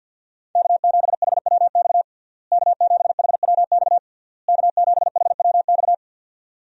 Morse code, heard as a keyed tone, about 50 wpm, 700 Hz. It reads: K7HWX K7HWX K7HWX
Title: USA Call Signs - 5 Characters - Repeated 3x - 50wpm